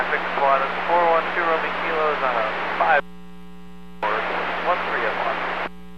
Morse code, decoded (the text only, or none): none